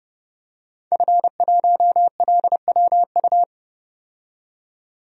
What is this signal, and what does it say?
Morse code, heard as a keyed tone, about 31 wpm, 700 Hz. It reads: F1LWU
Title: DXCC-100 Call Signs - 5 Characters - 30wpm